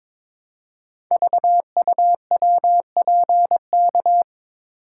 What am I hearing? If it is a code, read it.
Morse code, heard as a keyed tone, about 23 wpm, 700 Hz. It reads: VUWPK
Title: Code Groups - 5 Letters - 22wpm